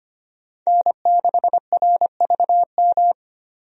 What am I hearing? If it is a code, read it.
Morse code, heard as a keyed tone, about 25 wpm, 690 Hz. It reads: N6RVM
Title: USA Call Signs - 5 Characters - 25wpm